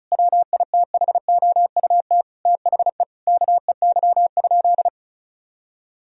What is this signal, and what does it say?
Morse code, heard as a keyed tone, about 34 wpm, 700 Hz. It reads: WITHOUT THE KEY?
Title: Sentences from Top 1000 Words - Review Part 4 - 35wpm